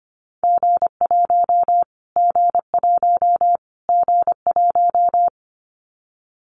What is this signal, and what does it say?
Morse code, heard as a keyed tone, about 25 wpm, 700 Hz. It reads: G1 G1 G1
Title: Callsign Prefixes for Top 25 DXCC - Repeated 3x - 25wpm